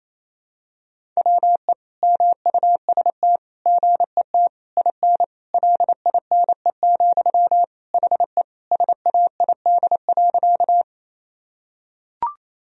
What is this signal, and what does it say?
Morse code, heard as a keyed tone, about 28 wpm, 700 Hz. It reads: WE MUST GET IN LINE, HE SAID.
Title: Sentences from Top 1000 Words - Review Part 1 - 28wpm